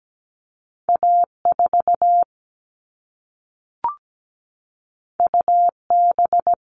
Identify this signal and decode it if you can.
Morse code, heard as a keyed tone, about 17 wpm, 700 Hz. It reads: A4 UB
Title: All Cut Numbers - 2-Digit 17wpm